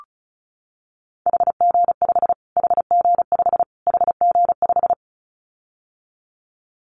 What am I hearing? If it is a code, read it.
Morse code, heard as a keyed tone, about 34 wpm, 700 Hz. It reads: HG5 HG5 HG5
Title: Callsign Prefixes for Top 25 DXCC - Repeated 3x - 35wpm